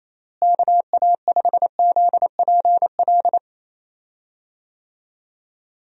Morse code, heard as a keyed tone, about 28 wpm, 700 Hz. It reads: KA5ZPL
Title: USA Call Signs - 6 Characters - Repeated 2x - 28wpm